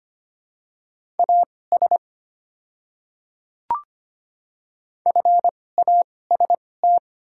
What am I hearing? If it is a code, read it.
Morse code, heard as a keyed tone, about 25 wpm, spaced as for 18 wpm, 700 Hz. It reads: AS FAST
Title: 2x Character Spacing - Top 500 Words - 25wpm